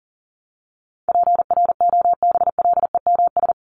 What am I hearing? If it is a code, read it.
Morse code, heard as a keyed tone, about 40 wpm, 700 Hz. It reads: PROBLEMS
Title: ES:S Suffix - Top 100 Words with Base Words 40wpm